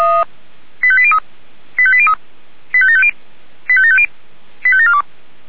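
Five-tone selective call (ZVEI), heard as quick runs of five tones, tones of about 70 ms. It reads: 85803 85803 85750 85750 85742
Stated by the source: ZVEI five-tone selective call received on an SDR.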